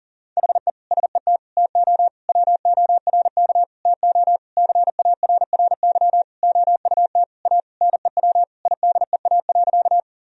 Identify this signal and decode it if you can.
Morse code, heard as a keyed tone, about 37 wpm, 700 Hz. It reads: HE SET TO WORK TO CARRY OUT A NEW IDEA.
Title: Sentences from Top 1000 Words - Review Part 1 - 40wpm